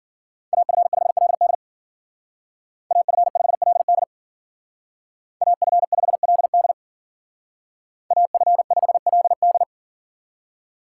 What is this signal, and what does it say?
Morse code, heard as a keyed tone, about 51 wpm, 700 Hz. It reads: AF5LD AF5LD AF5LD AF5LD